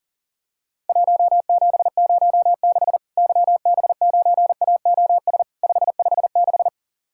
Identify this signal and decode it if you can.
Morse code, heard as a keyed tone, about 40 wpm, 700 Hz. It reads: 1706 YB9AOS 556
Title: Worked All Europe DX Contest - QTC Exchange - 40wpm